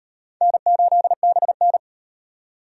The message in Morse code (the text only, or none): N8BN